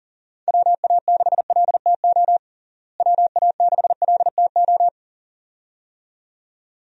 Morse code, heard as a keyed tone, about 40 wpm, 700 Hz. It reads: WA6LTO WA6LTO
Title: USA Call Signs - 6 Characters - Repeated 2x - 40wpm